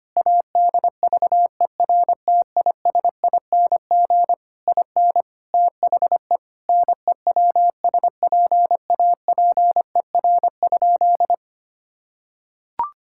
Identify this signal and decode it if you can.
Morse code, heard as a keyed tone, about 25 wpm, 700 Hz. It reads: ADVERTISING IN THE NEWSPAPER?
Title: Sentences from Top 1000 Words - Review Part 6 - 25wpm